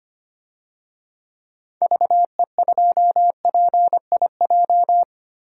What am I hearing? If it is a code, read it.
Morse code, heard as a keyed tone, about 25 wpm, 700 Hz. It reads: VE2PIJ
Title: DXCC-100 Call Signs - 6 Characters - 25wpm